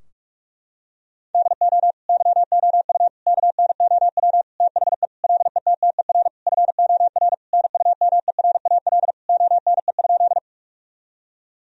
Morse code, heard as a keyed tone, about 46 wpm, 700 Hz. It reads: DO YOU KNOW THE LETTER FOR NUMERAL ONE?
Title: Sentences from Top 1000 Words - Review Part 2 - 45wpm